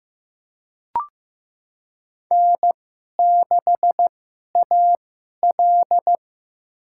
Morse code, heard as a keyed tone, about 15 wpm, spaced as for 11 wpm, 700 Hz. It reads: N6AL
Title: USA Call Signs - 2x Character Spacing - 15wpm